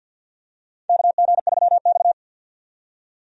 Morse code, heard as a keyed tone, about 50 wpm, 700 Hz. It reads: KG3X
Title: USA Call Signs - 4 Characters - 50wpm